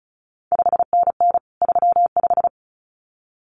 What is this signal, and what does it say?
Morse code, heard as a keyed tone, about 36 wpm, 700 Hz. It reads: 5NN 35
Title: CQ WW DX Contest Exchange Encore - 35wpm